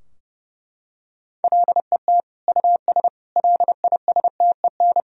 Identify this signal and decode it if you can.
Morse code, heard as a keyed tone, about 30 wpm, 700 Hz. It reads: LET US LISTEN
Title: Sentences from Top 1000 Words - Review Part 3 - 30wpm